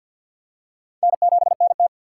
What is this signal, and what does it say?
Morse code, heard as a keyed tone, about 51 wpm, 700 Hz. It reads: N7NT
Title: K1USN SST QSO Practice - 50wpm